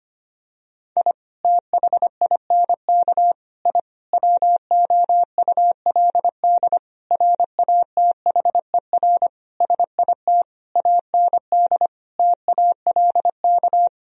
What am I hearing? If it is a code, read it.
Morse code, heard as a keyed tone, about 25 wpm, 700 Hz. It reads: I THINK I WOULD RATHER SIT AND TALK
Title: Sentences from Top 1000 Words - Review Part 4 - 25wpm